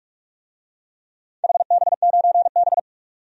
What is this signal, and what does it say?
Morse code, heard as a keyed tone, about 46 wpm, 700 Hz. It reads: HB9B
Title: DXCC-100 Call Signs - 4 Characters - 45wpm